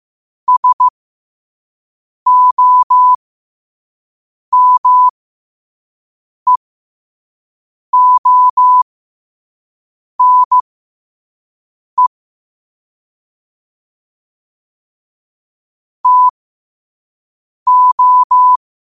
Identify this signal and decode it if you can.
Morse code, heard as a keyed tone, about 15 wpm, spaced as for 5 wpm, 1000 Hz. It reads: SOMEONE TO
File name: qotdfiles / 5wpm